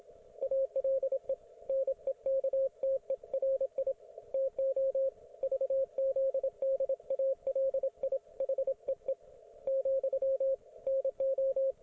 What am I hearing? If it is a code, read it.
Morse code, heard as a keyed tone, about 27 wpm, 540 Hz, with radio noise. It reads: ALE NEKTERI TO VZDALI HEE , NO